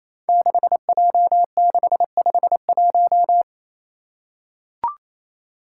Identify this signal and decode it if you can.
Morse code, heard as a keyed tone, about 28 wpm, 700 Hz. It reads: 6J651
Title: Five Challenging Letter-Number Combinations 28wpm